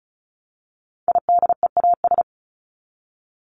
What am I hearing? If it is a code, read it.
Morse code, heard as a keyed tone, about 34 wpm, 700 Hz. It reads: IDEAS
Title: ES:S Suffix - Top 100 Words with Base Words 35wpm